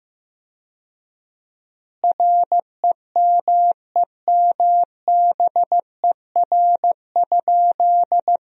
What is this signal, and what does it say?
Morse code, heard as a keyed tone, about 15 wpm, 700 Hz. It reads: REMEMBER?
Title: 4x Word-Spacing - Sentences from Top 400 Words - 15wpm